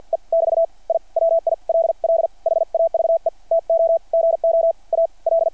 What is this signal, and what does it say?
Morse code, heard as a keyed tone, about 50 wpm, spaced as for 41 wpm, 660 Hz. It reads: E <BT> I WILL HAVE TO GO AL
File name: archived / CRSnet